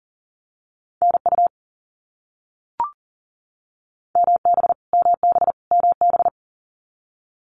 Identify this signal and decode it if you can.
Morse code, heard as a keyed tone, about 40 wpm, 700 Hz. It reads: NU MB MB MB
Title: Canadian Provinces and Territories - Repeated 3x - 40wpm